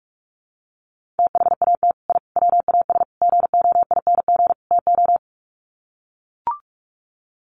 Morse code, heard as a keyed tone, about 44 wpm, 700 Hz. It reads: THAT I WAS GOING TO
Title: Sets of 5 Words Encore 45wpm